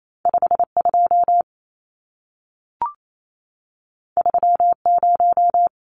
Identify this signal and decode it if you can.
Morse code, heard as a keyed tone, about 28 wpm, 700 Hz. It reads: 52 30
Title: Numbers - 2-Digit 28wpm